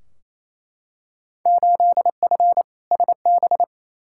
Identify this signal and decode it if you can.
Morse code, heard as a keyed tone, about 28 wpm, 700 Hz. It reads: 8F SB
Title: Field Day Exchanges Encore - 28wpm